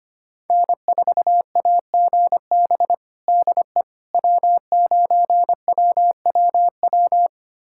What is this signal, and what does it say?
Morse code, heard as a keyed tone, about 25 wpm, 700 Hz. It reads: N4AGB DE W9WWW
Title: USA Callsign DE Callsign - 5 Characters - 25wpm